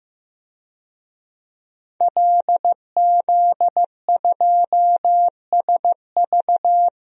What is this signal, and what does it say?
Morse code, heard as a keyed tone, about 16 wpm, 700 Hz. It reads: LZ2SV
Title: DXCC-100 Call Signs - 5 Characters - 15wpm